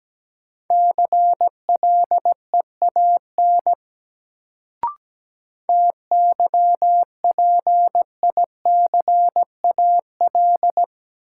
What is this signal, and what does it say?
Morse code, heard as a keyed tone, about 17 wpm, 700 Hz. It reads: CLEAN TYPICAL
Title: Sentences from Top 1100 Words - New Word Drill Alt - 17wpm